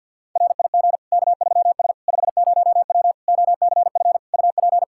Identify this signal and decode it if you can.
Morse code, heard as a keyed tone, about 53 wpm, 700 Hz. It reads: RIG K3S 50W OCF UP
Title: Practice QSO - Exchange 2 - Info - 50wpm